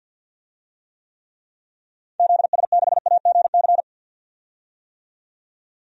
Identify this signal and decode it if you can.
Morse code, heard as a keyed tone, about 53 wpm, 700 Hz. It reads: ZS6AGC
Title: DXCC-100 Call Signs - 6 Characters - 50wpm